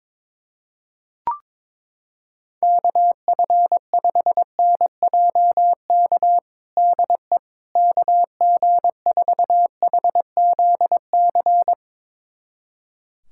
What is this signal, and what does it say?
Morse code, heard as a keyed tone, about 22 wpm, 700 Hz. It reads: KF5NJK DE KG4HZC
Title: USA Callsign DE Callsign - 6 Characters - 22wpm